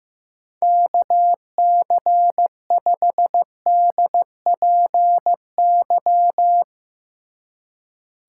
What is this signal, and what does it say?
Morse code, heard as a keyed tone, about 16 wpm, 700 Hz. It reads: KC5DPY
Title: Speed Racing - USA 2x3 Call Signs 15wpm